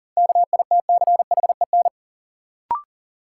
Morse code, heard as a keyed tone, about 40 wpm, 700 Hz. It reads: KITCHEN
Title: Sentences from Top 1400 Words - New Word Drill Alt - 40wpm